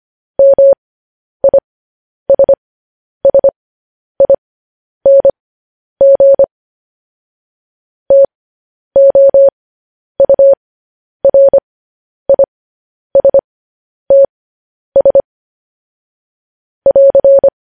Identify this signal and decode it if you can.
Morse code, heard as a keyed tone, about 25 wpm, spaced as for 10 wpm, 550 Hz. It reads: MISSING TOURISTS <AR>
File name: podcast / News headlines-1x-25x10